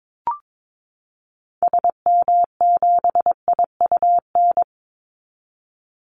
Morse code, heard as a keyed tone, about 22 wpm, 700 Hz. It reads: SM7IUN
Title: Call Signs Repeated 3x - 22wpm